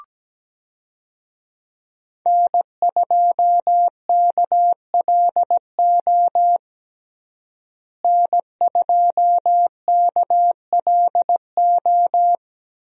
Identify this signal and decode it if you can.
Morse code, heard as a keyed tone, about 17 wpm, 700 Hz. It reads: N2KLO N2KLO
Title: USA Call Signs - 5 Characters - Repeated 3x - 17wpm